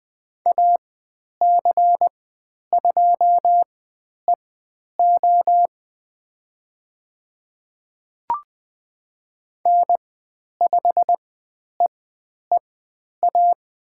Morse code, heard as a keyed tone, about 20 wpm, spaced as for 10 wpm, 700 Hz. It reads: AC2EO N5EEA
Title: 109 Lesson 18g - Letter m - Call Signs 20wpm